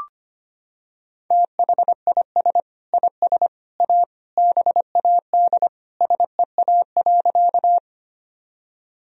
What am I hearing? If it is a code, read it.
Morse code, heard as a keyed tone, about 25 wpm, 700 Hz. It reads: THIS IS A BAD SEA.
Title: Sentences from Top 1000 Words - Review Part 1 - 25wpm